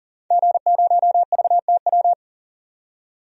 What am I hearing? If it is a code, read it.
Morse code, heard as a keyed tone, about 40 wpm, 700 Hz. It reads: G0VTW
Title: DXCC-100 Call Signs - 5 Characters - 40wpm